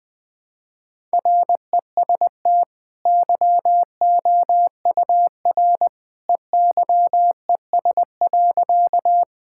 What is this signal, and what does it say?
Morse code, heard as a keyed tone, about 20 wpm, 700 Hz. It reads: REST YOUR EYES.
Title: Easy Sentences with Hints - 20wpm